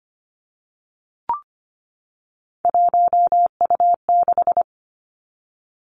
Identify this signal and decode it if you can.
Morse code, heard as a keyed tone, about 25 wpm, 700 Hz. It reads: 1U6
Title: Three Challenging Letter-Number Combinations 25wpm